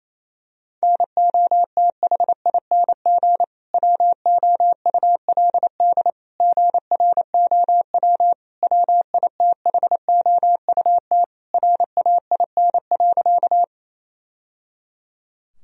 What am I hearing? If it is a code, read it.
Morse code, heard as a keyed tone, about 28 wpm, 700 Hz. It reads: NOTHING WOULD GROW WITHOUT RAIN.
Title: Sentences from Top 1000 Words - Review Part 3 - 28wpm